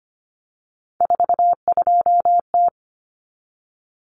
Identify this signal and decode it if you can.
Morse code, heard as a keyed tone, about 25 wpm, 700 Hz. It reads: 42T
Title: Common Cut Numbers - 3-Digit 25wpm